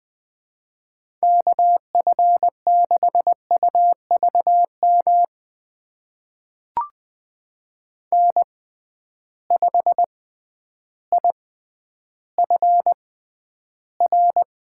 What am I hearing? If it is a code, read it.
Morse code, heard as a keyed tone, about 21 wpm, spaced as for 11 wpm, 700 Hz. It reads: KF6UVM N5IFR
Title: USA Call Signs - 6x Character Spacing - 20wpm